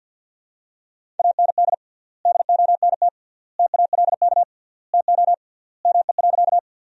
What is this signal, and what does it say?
Morse code, heard as a keyed tone, about 50 wpm, 700 Hz. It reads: AND DONT TALK TO ME.
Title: 3x Word-Spacing - Sentences from Top 400 Words - 50wpm